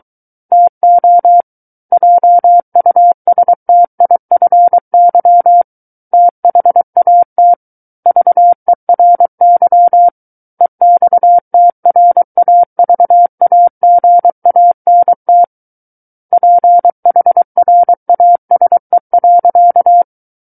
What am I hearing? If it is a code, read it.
Morse code, heard as a keyed tone, about 23 wpm, 700 Hz. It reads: TO JUSTIFY THAT VERY EXTRAVAGANT PHRASE.
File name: Book One / 14-Chapter 14 - 23 WPM